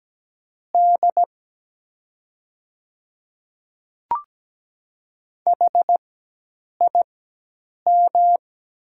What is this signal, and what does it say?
Morse code, heard as a keyed tone, about 17 wpm, spaced as for 8 wpm, 700 Hz. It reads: D HIM
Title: 4x Character Spacing - Top 100 Words - 17wpm